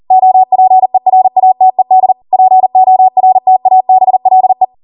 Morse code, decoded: OPERATED PORTABLE